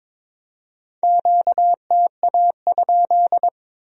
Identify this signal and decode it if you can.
Morse code, heard as a keyed tone, about 26 wpm, 700 Hz. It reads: QTA?
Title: Speed-Racing Q-Codes - 22wpm